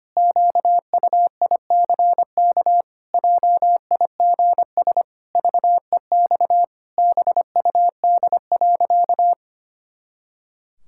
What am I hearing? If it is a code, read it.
Morse code, heard as a keyed tone, about 25 wpm, 700 Hz. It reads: QUICK JIGS VEX BUD.
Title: Pangram Challenge v2 - 25wpm